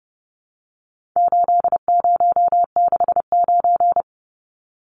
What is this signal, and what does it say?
Morse code, heard as a keyed tone, about 30 wpm, 700 Hz. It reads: 8069
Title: Numbers - 4-Digit 30wpm